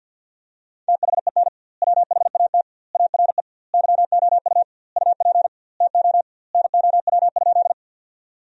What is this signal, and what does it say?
Morse code, heard as a keyed tone, about 52 wpm, 700 Hz. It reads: THEN WHAT ARE YOU UP TO NOW?
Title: 2x Word-Spacing - Sentences from Top 100 Words - 50wpm